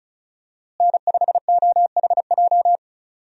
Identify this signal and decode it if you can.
Morse code, heard as a keyed tone, about 34 wpm, 700 Hz. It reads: N5OHJ
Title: USA Call Signs - 5 Characters - Repeated 3x - 35wpm